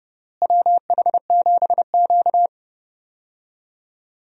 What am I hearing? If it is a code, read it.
Morse code, heard as a keyed tone, about 30 wpm, 700 Hz. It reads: WH7Q
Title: USA Call Signs - 4 Characters - Repeated 3x - 30wpm